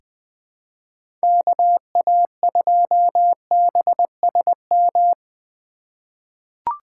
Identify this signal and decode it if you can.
Morse code, heard as a keyed tone, about 20 wpm, 700 Hz. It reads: KA2BSM
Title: USA Call Signs - 2x Character Spacing - 20wpm